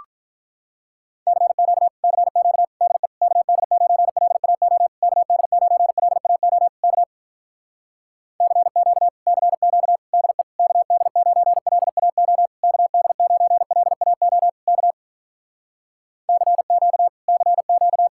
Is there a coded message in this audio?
Yes: Morse code from 1.3 to 18.1 s, heard as a keyed tone, about 46 wpm, 700 Hz. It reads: CQ CQ DE KD9LAO KD9LAO K CQ CQ DE KD9LAO KD9LAO K CQ CQ